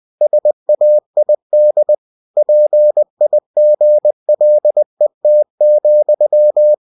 Morse code, heard as a keyed tone, about 20 wpm, 600 Hz. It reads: SAID PIGLET,
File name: cw-books / winnie-the-pooh / winnie-20wpm-page-0063